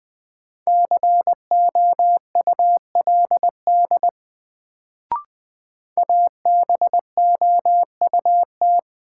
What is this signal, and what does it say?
Morse code, heard as a keyed tone, about 20 wpm, 700 Hz. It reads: COULD ABOUT
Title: Sets of 2 Words - New Word Drill - 20wpm